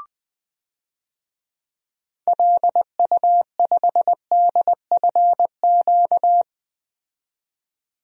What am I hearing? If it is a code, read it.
Morse code, heard as a keyed tone, about 20 wpm, 700 Hz. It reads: LU5DFQ
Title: DXCC-100 Call Signs - 6 Characters - 20wpm